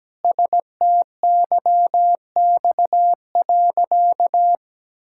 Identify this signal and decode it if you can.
Morse code, heard as a keyed tone, about 17 wpm, 700 Hz. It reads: STYX.
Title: Pangram Challenge v2 - 17wpm